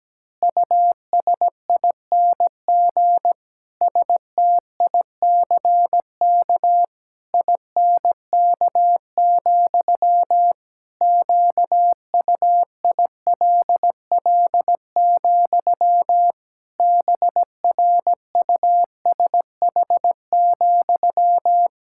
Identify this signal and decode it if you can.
Morse code, heard as a keyed tone, about 17 wpm, 700 Hz. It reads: USING STICK INK, QUILL, BRUSH,
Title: Pangram Challenge v2 - 17wpm